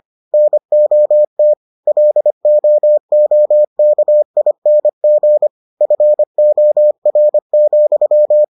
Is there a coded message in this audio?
Yes: Morse code from 0.3 to 8.4 s, heard as a keyed tone, about 25 wpm, 600 Hz. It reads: NOT LOOKING FOR,